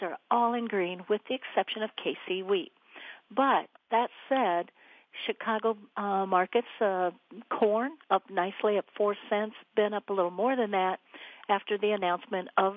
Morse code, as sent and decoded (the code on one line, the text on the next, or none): none